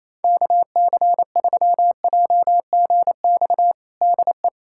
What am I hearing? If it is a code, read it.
Morse code, heard as a keyed tone, about 28 wpm, 700 Hz. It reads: KC3JGX DE
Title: USA Callsign DE Callsign - 6 Characters - 28wpm